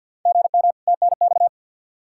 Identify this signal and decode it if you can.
Morse code, heard as a keyed tone, about 51 wpm, 700 Hz. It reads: GM TNX
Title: Practice QSO - Exchange 1 - Beginning - 50wpm